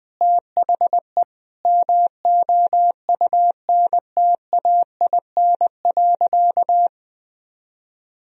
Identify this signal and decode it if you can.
Morse code, heard as a keyed tone, about 21 wpm, 700 Hz. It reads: THE MOUNTAIN.
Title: Sentences from Top 1000 Words - Review Part 1 - 20wpm